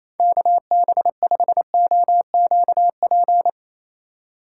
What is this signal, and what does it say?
Morse code, heard as a keyed tone, about 27 wpm, 700 Hz. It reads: KB5OQP
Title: USA Call Signs - 6 Characters - 3x Character Spacing - 28wpm